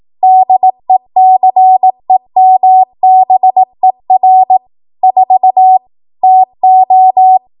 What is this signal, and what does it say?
Morse code, heard as a keyed tone, about 18 wpm, 750 Hz. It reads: DECEMBER 4 TO
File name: Archive / 18 WPM / 18WPM